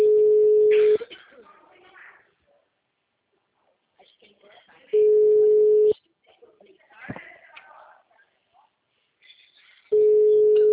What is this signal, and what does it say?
Ringback tone (Europe, 425 Hz), with speech.
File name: out- -2009- - - .149